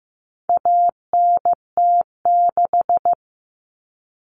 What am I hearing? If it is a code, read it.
Morse code, heard as a keyed tone, about 15 wpm, 700 Hz. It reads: ANT6
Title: Common Cut Numbers - 4-Digit 15wpm